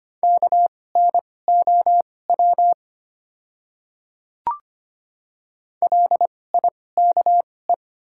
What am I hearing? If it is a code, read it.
Morse code, heard as a keyed tone, about 25 wpm, spaced as for 18 wpm, 700 Hz. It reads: KNOW LIKE
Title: 2x Character Spacing - Top 100 Words - 25wpm